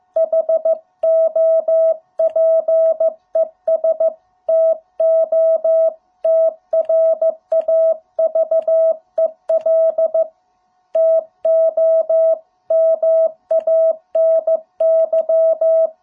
Morse code, heard as a keyed tone, about 15 wpm, 650 Hz. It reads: HOPES TO TRAVEL TO MANY